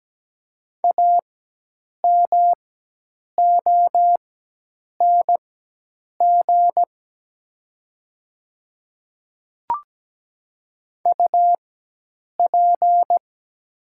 Morse code, heard as a keyed tone, about 17 wpm, spaced as for 8 wpm, 700 Hz. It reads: AMONG UP
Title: 4x Character Spacing - Top 500 Words - 17wpm